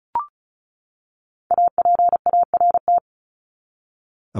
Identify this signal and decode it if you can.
Morse code, heard as a keyed tone, about 34 wpm, 700 Hz. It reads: APART
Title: Top 1501-1750 Words 35wpm